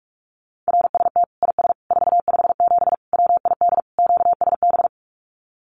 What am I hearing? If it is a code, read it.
Morse code, heard as a keyed tone, about 44 wpm, 700 Hz. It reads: RST IS 457 WID QSB
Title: Common QSO Phrases 45wpm